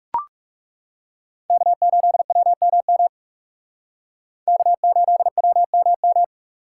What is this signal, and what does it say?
Morse code, heard as a keyed tone, about 37 wpm, 700 Hz. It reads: K8WMM K8WMM
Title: Speed Racing - USA 1x3 Call Signs 30wpm